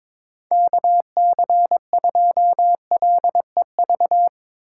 Morse code, heard as a keyed tone, about 22 wpm, 700 Hz. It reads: KC2LEV
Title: USA Call Signs - 6 Characters - 6x Character Spacing - 22wpm